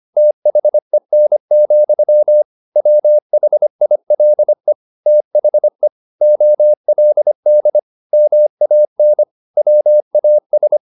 Morse code, heard as a keyed tone, about 25 wpm, 600 Hz. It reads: THEN, WHILE THE OLD MAN WAS
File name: the old man and the sea part 1 25wpm